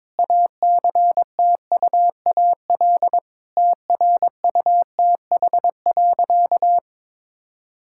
Morse code, heard as a keyed tone, about 22 wpm, 700 Hz. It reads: ACTUAL TRUTH.
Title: Sentences from Top 1000 Words - Review Part 5 - 22wpm